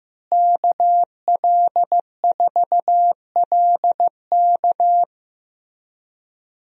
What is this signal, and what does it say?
Morse code, heard as a keyed tone, about 15 wpm, 700 Hz. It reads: KL4LK
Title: USA Call Signs - 5 Characters - Repeated 3x - 15wpm